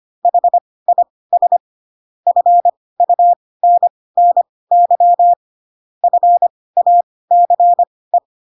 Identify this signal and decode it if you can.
Morse code, heard as a keyed tone, about 25 wpm, spaced as for 18 wpm, 700 Hz. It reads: HIS FUNNY FACE
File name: Pinocchio / Pinocchio ch1 12 25 ik1hge